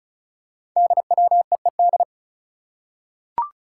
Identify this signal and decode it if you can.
Morse code, heard as a keyed tone, about 36 wpm, 700 Hz. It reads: DWEED
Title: Code Groups - 5 Letter:Numbers - 35wpm